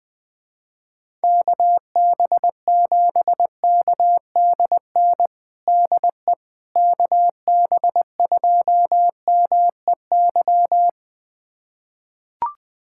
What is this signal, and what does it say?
Morse code, heard as a keyed tone, about 20 wpm, 700 Hz. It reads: KB7KDN DE KB2MEY
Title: USA Callsign DE Callsign - 6 Characters - 20wpm